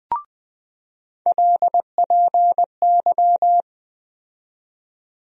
Morse code, heard as a keyed tone, about 21 wpm, 700 Hz. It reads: LPY
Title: USA Call Sign Postfixes - 3 Letters 20wpm